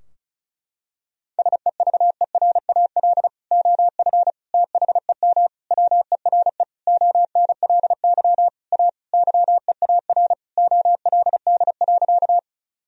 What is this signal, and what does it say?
Morse code, heard as a keyed tone, about 34 wpm, 700 Hz. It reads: SEVERAL OF THEM WERE ONLY A YEAR OLD.
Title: Sentences from Top 1000 Words - Review Part 2 - 35wpm